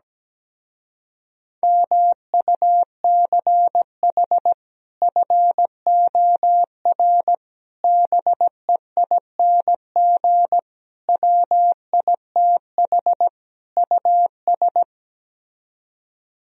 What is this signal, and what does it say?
Morse code, heard as a keyed tone, about 17 wpm, 700 Hz. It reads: MUCH FOR BEING WITH US
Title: Speed-Racing - Sets of 5 Words 17wpm